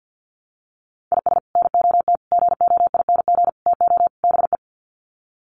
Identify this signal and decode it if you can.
Morse code, heard as a keyed tone, about 50 wpm, 700 Hz. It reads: IS NOT GOING TO BE
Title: Sets of 5 Words Encore2 50wpm